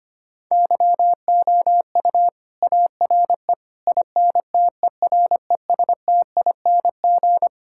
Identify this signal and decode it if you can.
Morse code, heard as a keyed tone, about 25 wpm, 700 Hz. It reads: YOU ARE INTERESTING
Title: Sentences from Top 1000 Words - Review Part 6 - 25wpm